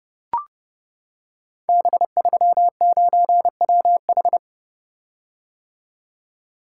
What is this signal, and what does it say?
Morse code, heard as a keyed tone, about 30 wpm, 700 Hz. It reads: B39WH
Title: Code Groups - 5 Letter:Numbers - 30wpm